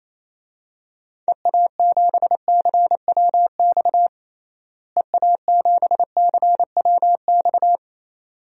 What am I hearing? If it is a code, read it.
Morse code, heard as a keyed tone, about 28 wpm, 700 Hz. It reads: EA7CWX EA7CWX
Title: DXCC-100 Call Signs - 6 Characters - Repeated 2x - 28wpm